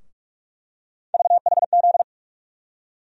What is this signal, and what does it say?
Morse code, heard as a keyed tone, about 41 wpm, 700 Hz. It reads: VHZ
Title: USA Call Sign Postfixes - 3 Letters 45wpm